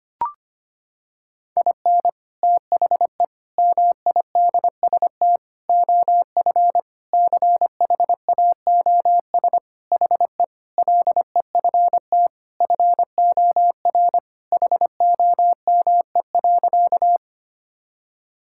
Morse code, heard as a keyed tone, about 25 wpm, 700 Hz. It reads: IN THE MIDST OF CHAOS HE LEFT FOR HOME.
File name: practice / 25wpm / Sentences with 4-Word Phrases - 25wpm